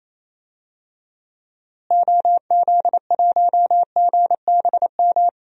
Morse code, heard as a keyed tone, about 27 wpm, 700 Hz. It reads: OZ1GBM
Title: DXCC-100 Call Signs - 6 Characters - Repeated 3x - 28wpm